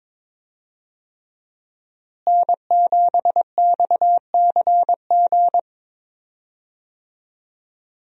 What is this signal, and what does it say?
Morse code, heard as a keyed tone, about 22 wpm, 700 Hz. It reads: N7XCG
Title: USA Call Signs - 5 Characters - Repeated 3x - 22wpm